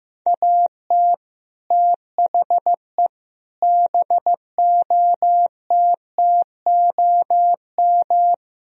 Morse code, heard as a keyed tone, about 15 wpm, 700 Hz. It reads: AT THE BOTTOM
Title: Speed-Racing - Sets of 4 Words 15wpm